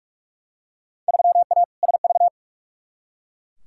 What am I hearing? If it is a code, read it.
Morse code, heard as a keyed tone, about 43 wpm, 700 Hz. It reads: 3A SV
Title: Field Day Exchanges - 45wpm